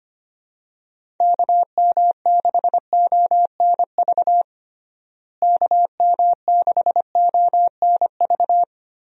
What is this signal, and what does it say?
Morse code, heard as a keyed tone, about 25 wpm, 700 Hz. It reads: KM6ONV KM6ONV
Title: USA Call Signs - 6 Characters - Repeated 3x - 25wpm